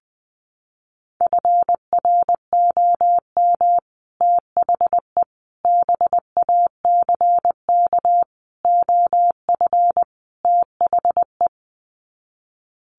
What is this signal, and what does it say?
Morse code, heard as a keyed tone, about 19 wpm, 700 Hz. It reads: FROM THE BACK OF THE
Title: Sets of 5 Words Encore 20wpm